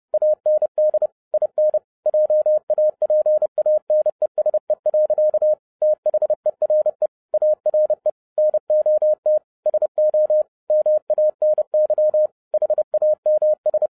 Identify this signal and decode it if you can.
Morse code, heard as a keyed tone, about 30 wpm, 610 Hz. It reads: AND IN JAPANESE. THERE ARE NOT SO MANY HAMS